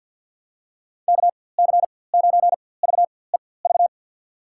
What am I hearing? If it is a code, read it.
Morse code, heard as a keyed tone, about 53 wpm, spaced as for 24 wpm, 700 Hz. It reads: KC9VEV
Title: USA Call Signs - 6 Characters - 4x Character Spacing - 50wpm